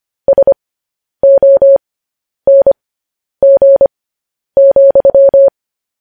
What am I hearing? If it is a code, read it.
Morse code, heard as a keyed tone, about 25 wpm, spaced as for 10 wpm, 550 Hz. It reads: SONG,
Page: news headlines in morse code at 10 wpm